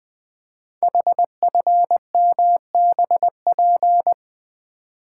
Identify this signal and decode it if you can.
Morse code, heard as a keyed tone, about 20 wpm, 700 Hz. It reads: HFMBP
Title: Code Groups - 5 Letter:Numbers - 20wpm